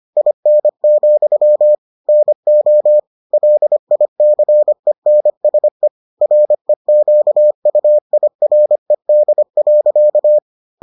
Morse code, heard as a keyed tone, about 25 wpm, 600 Hz. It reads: IN, NO LICENSE REQUIRED.